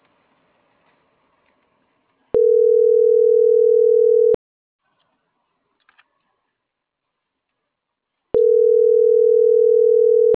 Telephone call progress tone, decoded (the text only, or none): ringback tone